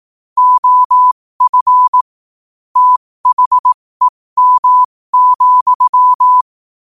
Morse code, heard as a keyed tone, about 18 wpm, spaced as for 15 wpm, 1000 Hz. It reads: OF THEM,
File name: qotdfiles / 15wpm